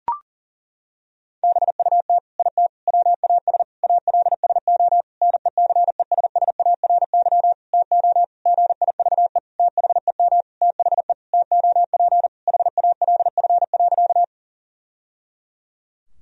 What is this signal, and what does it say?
Morse code, heard as a keyed tone, about 40 wpm, 700 Hz. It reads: BUT IT WAS ALSO NECESSARY TO GIVE THEM THE TOP HALF.
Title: Sentences from Top 1000 Words - Review Part 2 - 40wpm